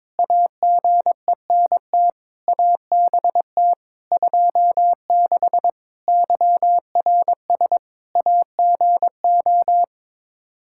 Morse code, heard as a keyed tone, about 22 wpm, 700 Hz. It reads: AGENT ABT 26 YRS AGO
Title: Practice QSO - Exchange 3 - More Info - 22wpm